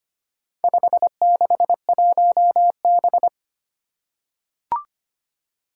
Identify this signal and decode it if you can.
Morse code, heard as a keyed tone, about 25 wpm, 700 Hz. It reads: 561B
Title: Four Challenging Letter-Number Combinations 25wpm